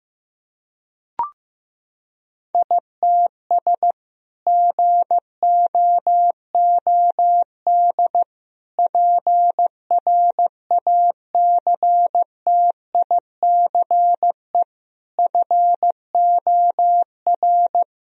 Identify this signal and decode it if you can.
Morse code, heard as a keyed tone, about 15 wpm, 700 Hz. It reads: ITS GOOD PRACTICE FOR
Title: Sentences from Top 1000 Words - Review Part 4 - 15wpm